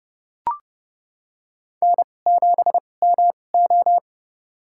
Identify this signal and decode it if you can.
Morse code, heard as a keyed tone, about 29 wpm, spaced as for 21 wpm, 700 Hz. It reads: N7MO
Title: USA Call Signs - 4 Characters - 2x Character Spacing - 30wpm